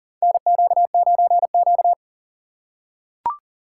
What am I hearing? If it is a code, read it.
Morse code, heard as a keyed tone, about 40 wpm, 700 Hz. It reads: NQ9Q
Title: USA Call Signs - 4 Characters - 2x Character Spacing - 40wpm